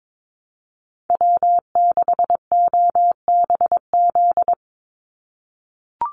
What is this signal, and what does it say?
Morse code, heard as a keyed tone, about 22 wpm, 700 Hz. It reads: W6OBZ
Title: USA Call Signs 22wpm